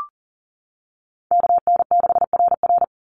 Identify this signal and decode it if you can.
Morse code, heard as a keyed tone, about 40 wpm, 700 Hz. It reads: KN6RR
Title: USA Call Signs 40wpm